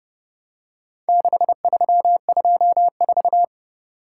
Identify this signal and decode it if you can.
Morse code, heard as a keyed tone, about 29 wpm, 700 Hz. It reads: 6324